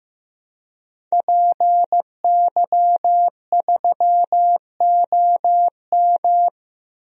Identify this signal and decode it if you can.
Morse code, heard as a keyed tone, about 15 wpm, 700 Hz. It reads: PY3OM
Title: DXCC-100 Call Signs - 5 Characters - 15wpm